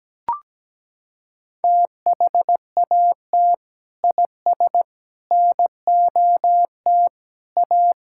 Morse code, heard as a keyed tone, about 17 wpm, 700 Hz. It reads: THAT IS NOT A
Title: Sentences from Top 1000 Words - Review Part 5 - 17wpm